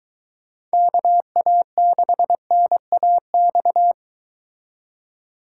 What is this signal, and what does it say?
Morse code, heard as a keyed tone, about 23 wpm, 700 Hz. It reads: KA6NAX